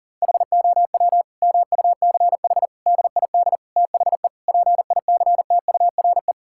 Morse code, heard as a keyed tone, about 40 wpm, 690 Hz. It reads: HOW MUCH DID THE PICTURE
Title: Sentences from Top 1000 Words - Review Part 6 - 40wpm